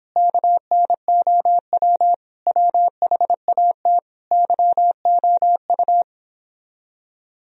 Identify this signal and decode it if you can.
Morse code, heard as a keyed tone, about 29 wpm, 700 Hz. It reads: KNOW WHAT YOU